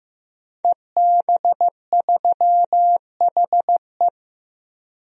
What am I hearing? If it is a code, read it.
Morse code, heard as a keyed tone, about 15 wpm, 700 Hz. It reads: EB3HE
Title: DXCC-100 Call Signs - 5 Characters - 15wpm